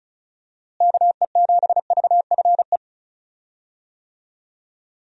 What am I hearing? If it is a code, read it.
Morse code, heard as a keyed tone, about 36 wpm, 700 Hz. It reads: KE7VFE
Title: USA Call Signs - 6 Characters - Repeated 3x - 35wpm